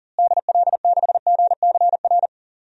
Morse code, heard as a keyed tone, about 40 wpm, 700 Hz. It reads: DL6GCR
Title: DXCC-100 Call Signs - 6 Characters - Repeated 3x - 40wpm